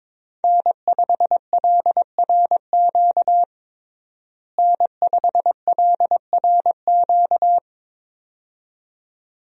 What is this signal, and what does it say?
Morse code, heard as a keyed tone, about 22 wpm, 700 Hz. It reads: N5LRQ N5LRQ
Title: USA Call Signs - 5 Characters - Repeated 2x - 22wpm